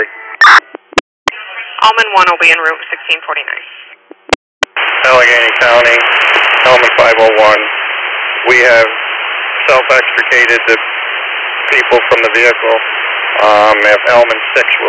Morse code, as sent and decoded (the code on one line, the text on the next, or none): none